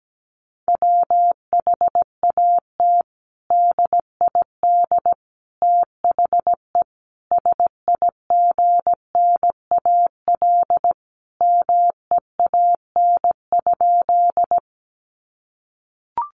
Morse code, heard as a keyed tone, about 17 wpm, 700 Hz. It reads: WHAT DID THE SIGNAL MEAN?
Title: Sentences from Top 1400 Words 17wpm